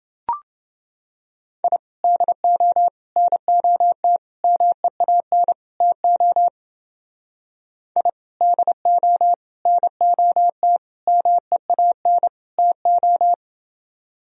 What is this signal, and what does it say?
Morse code, heard as a keyed tone, about 29 wpm, 700 Hz. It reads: I DO NOT MEAN TO I DO NOT MEAN TO